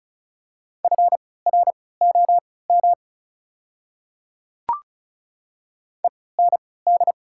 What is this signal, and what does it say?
Morse code, heard as a keyed tone, about 36 wpm, spaced as for 20 wpm, 690 Hz. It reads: FROM END
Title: 3x Character Spacing - Top 200 Words - 35wpm